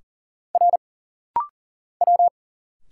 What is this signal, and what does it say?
Morse code, heard as a keyed tone, about 40 wpm, 700 Hz. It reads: R W
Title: Single Letters - Rapid-Fire 40wpm